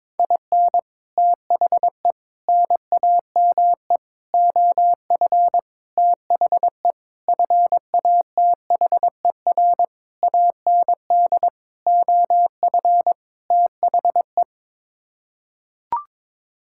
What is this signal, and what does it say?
Morse code, heard as a keyed tone, about 22 wpm, 700 Hz. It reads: IN THE NAME OF THE FATHER AND OF THE
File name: Sets of 9 Words - 22wpm